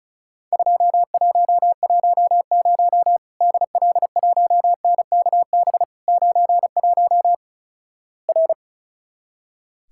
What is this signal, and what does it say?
Morse code, heard as a keyed tone, about 35 wpm, 700 Hz. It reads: 2110 DL1NKB 91 R
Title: Worked All Europe DX Contest - QTC Exchange - 35wpm